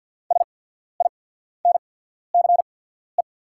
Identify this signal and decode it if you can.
Morse code, heard as a keyed tone, about 48 wpm, spaced as for 14 wpm, 690 Hz. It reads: SINCE
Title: 8x Character Spacing - Top 300 Words - 50wpm